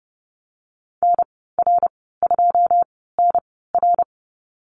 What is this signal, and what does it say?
Morse code, heard as a keyed tone, about 30 wpm, spaced as for 17 wpm, 700 Hz. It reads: NR2NR
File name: Lesson 14g - Number 2 - Call Signs 30wpm